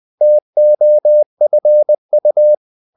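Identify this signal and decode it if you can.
Morse code, heard as a keyed tone, about 20 wpm, 600 Hz. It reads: TOFU
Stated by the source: Morse code practice of short Danish words at 20 wpm, from CW med Gnister.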